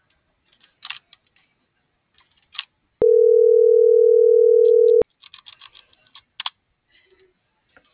Ringback tone (North America), 440 + 480 Hz.